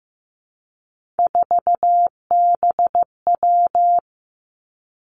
Morse code, heard as a keyed tone, about 15 wpm, 700 Hz. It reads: 4BW
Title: Three Challenging Letter-Number Combinations 15wpm